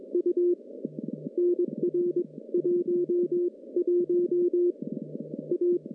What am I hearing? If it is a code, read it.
Morse code, heard as a keyed tone, about 22 wpm, 350 Hz. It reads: U NR 1 1 A
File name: contests / CW-SS-k5tr- -1200